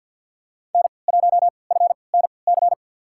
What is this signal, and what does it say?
Morse code, heard as a keyed tone, about 47 wpm, spaced as for 28 wpm, 700 Hz. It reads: N1FNC
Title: USA Call Signs - 5 Characters - 3x Character Spacing - 50wpm